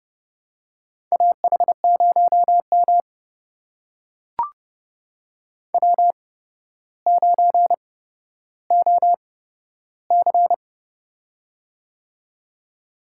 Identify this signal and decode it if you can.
Morse code, heard as a keyed tone, about 30 wpm, spaced as for 13 wpm, 700 Hz. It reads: AH0M W9OC
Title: USA Call Signs - 4 Characters - 8x Character Spacing - 30wpm